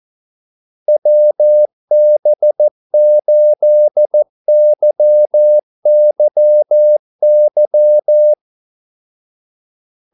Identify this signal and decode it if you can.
Morse code, heard as a keyed tone, about 14 wpm, 600 Hz. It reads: WB8YYY
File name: Callsigns-14wpm